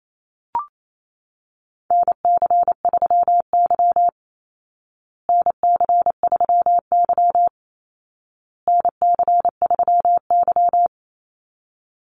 Morse code, heard as a keyed tone, about 28 wpm, 700 Hz. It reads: NC3Y NC3Y NC3Y
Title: Call Signs Repeated 2x - 28wpm